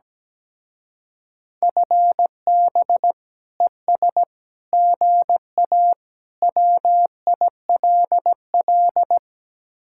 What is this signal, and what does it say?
Morse code, heard as a keyed tone, about 17 wpm, 700 Hz. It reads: FB ES GA WILL
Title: Speed-Racing Common QSO Phrases - 17wpm